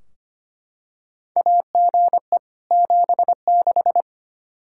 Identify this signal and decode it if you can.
Morse code, heard as a keyed tone, about 25 wpm, 700 Hz. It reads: AGE 76
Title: Practice QSO - Exchange 3 - More Info - 25wpm